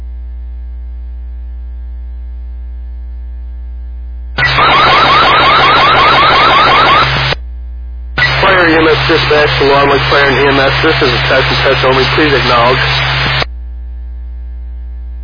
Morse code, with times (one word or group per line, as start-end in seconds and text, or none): none